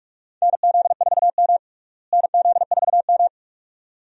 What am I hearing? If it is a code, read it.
Morse code, heard as a keyed tone, about 46 wpm, 700 Hz. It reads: NZ4M NZ4M
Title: USA Call Signs - 4 Characters - Repeated 2x - 45wpm